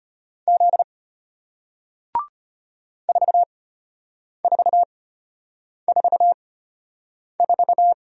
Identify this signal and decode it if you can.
Morse code, heard as a keyed tone, about 33 wpm, spaced as for 8 wpm, 700 Hz. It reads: Z 4444